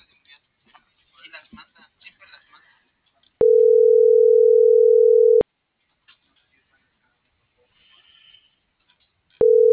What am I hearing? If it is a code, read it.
Ringback tone (North America), 440 + 480 Hz.